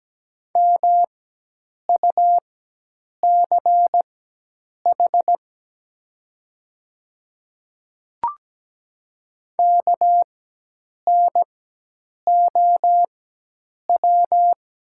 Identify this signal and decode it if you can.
Morse code, heard as a keyed tone, about 17 wpm, spaced as for 8 wpm, 700 Hz. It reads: MUCH KNOW
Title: 4x Character Spacing - Top 200 Words - 17wpm